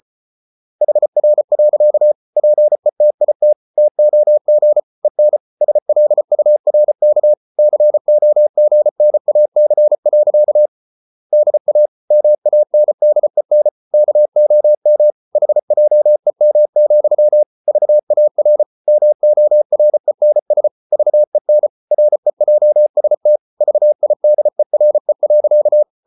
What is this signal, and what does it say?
Morse code, heard as a keyed tone, about 34 wpm, 600 Hz. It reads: HR. PETIT TOG EN SLURK COGNAC. DA MANDEN KOM HJEM, VAR MORENS VEN REJST VIDERE.
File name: Ferie QRQ 34wpm